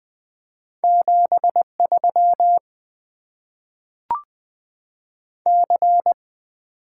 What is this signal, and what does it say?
Morse code, heard as a keyed tone, about 20 wpm, 700 Hz. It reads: 73 C
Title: Top 20 QSO Elements - 8x Character Spacing - 20wpm